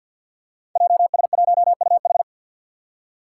Morse code, heard as a keyed tone, about 51 wpm, 700 Hz. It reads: JS1UH
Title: Five Challenging Letter-Number Combinations 50wpm